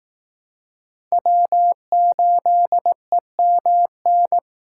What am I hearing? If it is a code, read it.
Morse code, heard as a keyed tone, about 18 wpm, 700 Hz. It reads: W8EMN
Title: Speed Racing - USA Call Signs 15wpm